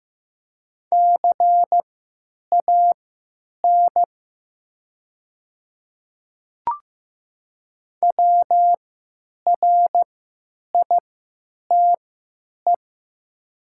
Morse code, heard as a keyed tone, about 15 wpm, spaced as for 9 wpm, 700 Hz. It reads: CAN WRITE
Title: 3x Character Spacing - Top 100 Words - 15wpm